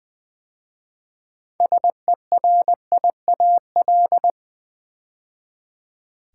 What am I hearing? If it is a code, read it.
Morse code, heard as a keyed tone, about 20 wpm, 700 Hz. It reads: SERIAL
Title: 6 Letter Words - First 3 Letters Given - 20wpm